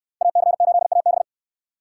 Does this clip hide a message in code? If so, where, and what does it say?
Yes: Morse code from 0.2 to 1.2 s, heard as a keyed tone, about 65 wpm, 700 Hz. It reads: AC8AB